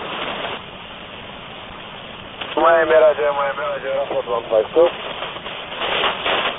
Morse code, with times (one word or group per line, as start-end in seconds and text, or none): none